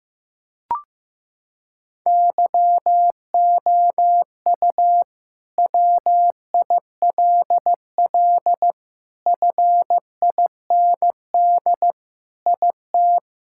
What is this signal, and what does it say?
Morse code, heard as a keyed tone, about 15 wpm, 700 Hz. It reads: YOU WILL FIND IT
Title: Sentences from Top 1000 Words - Review Part 5 - 15wpm